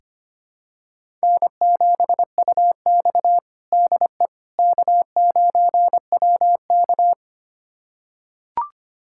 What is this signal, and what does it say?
Morse code, heard as a keyed tone, about 25 wpm, 700 Hz. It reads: N7UX DE K9WK
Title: USA Callsign DE Callsign - 4 Characters - 25wpm